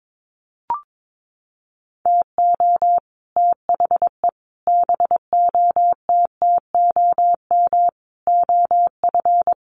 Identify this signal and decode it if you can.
Morse code, heard as a keyed tone, about 22 wpm, 700 Hz. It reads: TO THE BOTTOM OF
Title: Sets of 5 Words 22wpm